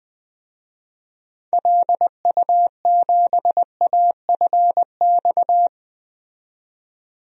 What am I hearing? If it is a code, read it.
Morse code, heard as a keyed tone, about 20 wpm, 700 Hz. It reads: LU7AFX
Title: DXCC-100 Call Signs - 6 Characters - 20wpm